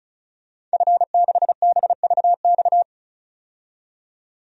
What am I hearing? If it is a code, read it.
Morse code, heard as a keyed tone, about 35 wpm, 700 Hz. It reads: F6BVX
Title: DXCC-100 Call Signs - 5 Characters - 35wpm